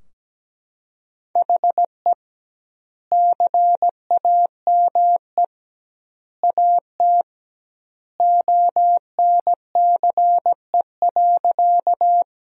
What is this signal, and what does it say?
Morse code, heard as a keyed tone, about 17 wpm, 700 Hz. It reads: HE CAME AT ONCE.
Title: 2x Word-Spacing - Sentences from Top 300 Words - 17wpm